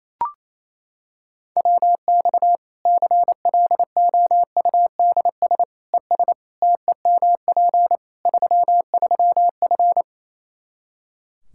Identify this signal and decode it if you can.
Morse code, heard as a keyed tone, about 27 wpm, 700 Hz. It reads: WX CLOUDS ES TEMP 33F
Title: K1USN SST QSO Practice 28wpm